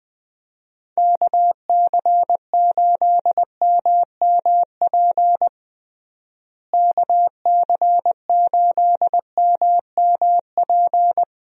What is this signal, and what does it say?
Morse code, heard as a keyed tone, about 21 wpm, 700 Hz. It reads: KC8MMP KC8MMP
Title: USA Call Signs - 6 Characters - Repeated 3x - 20wpm